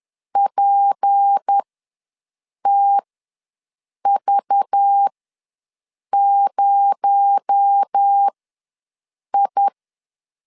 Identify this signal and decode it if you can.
Morse code, heard as a keyed tone, about 10 wpm, spaced as for 6 wpm, 800 Hz. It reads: PTV0I
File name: LEZIONI / 7 GF9Z / 7 G ws 10wpm 3843